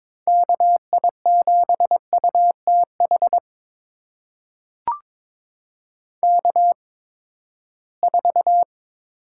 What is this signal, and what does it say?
Morse code, heard as a keyed tone, about 22 wpm, spaced as for 15 wpm, 700 Hz. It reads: KI7UTH K4
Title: USA Call Signs - 8x Character Spacing - 22wpm